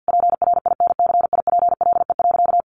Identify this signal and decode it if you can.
Morse code, heard as a keyed tone, about 50 wpm, 700 Hz. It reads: PRINCIPLE.
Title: Sentences from Top 1400 Words 50wpm